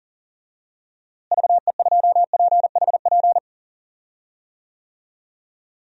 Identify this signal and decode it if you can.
Morse code, heard as a keyed tone, about 40 wpm, 700 Hz. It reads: VE2PHP
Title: DXCC-100 Call Signs - 6 Characters - Repeated 2x - 40wpm